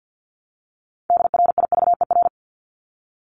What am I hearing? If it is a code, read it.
Morse code, heard as a keyed tone, about 48 wpm, 700 Hz. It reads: DRIVER
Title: Top 751-1000 Words 50wpm